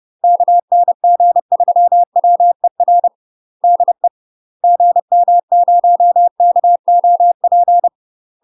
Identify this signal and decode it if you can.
Morse code, heard as a keyed tone, about 30 wpm, 700 Hz. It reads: KN G3WER DE GM0KOP
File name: UK MASTER QSOs TRAINING / 30WPM / UK MASTER QSOs 0017